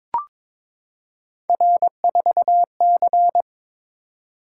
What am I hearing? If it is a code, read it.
Morse code, heard as a keyed tone, about 22 wpm, 700 Hz. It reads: R4C
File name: practice / 22wpm / Three Letter-Number Combinations - 22wpm